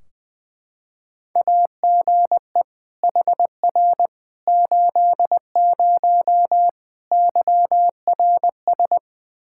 Morse code, heard as a keyed tone, about 21 wpm, 700 Hz. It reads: AGE HR 80 YRS
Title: K1USN SST QSO Practice 20wpm